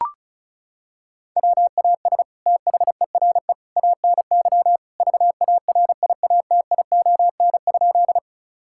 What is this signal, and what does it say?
Morse code, heard as a keyed tone, about 36 wpm, 700 Hz. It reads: WAS THERE ANY VARIATION?
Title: Sentences from Top 1600 Words 35wpm